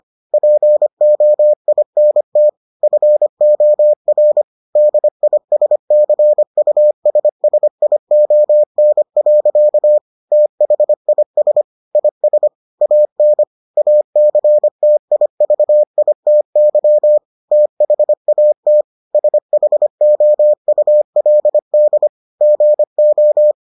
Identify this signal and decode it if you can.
Morse code, heard as a keyed tone, about 25 wpm, 600 Hz. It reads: POINT FOR DISCUSSION. THIS IS AN ACTIVITY THAT SHOULD GO